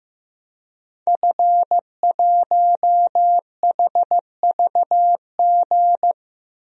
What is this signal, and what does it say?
Morse code, heard as a keyed tone, about 16 wpm, 700 Hz. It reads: F1HVG
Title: DXCC-100 Call Signs - 5 Characters - Repeated 2x - 15wpm